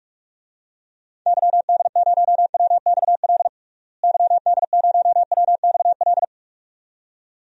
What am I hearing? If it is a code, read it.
Morse code, heard as a keyed tone, about 41 wpm, 700 Hz. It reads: YD0WXL YD0WXL
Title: DXCC-100 Call Signs - 6 Characters - Repeated 2x - 45wpm